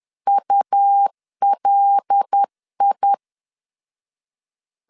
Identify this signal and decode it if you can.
Morse code, heard as a keyed tone, about 10 wpm, 800 Hz. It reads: ULI
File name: CW / DET / 1 6 1500C S B 10wpm 4137